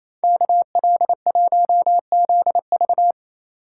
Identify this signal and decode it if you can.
Morse code, heard as a keyed tone, about 35 wpm, 700 Hz. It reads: KL1ZV
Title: Speed Racing - USA 2x2 Call Signs 28wpm